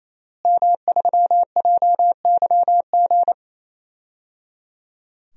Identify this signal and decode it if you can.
Morse code, heard as a keyed tone, about 27 wpm, 700 Hz. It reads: M3JYG
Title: DXCC-100 Call Signs - 5 Characters - Repeated 3x - 28wpm